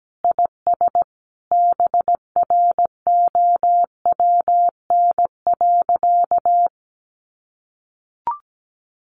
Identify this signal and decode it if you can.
Morse code, heard as a keyed tone, about 17 wpm, 700 Hz. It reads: IS BROWN.
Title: Sentences from Top 1200 Words 17wpm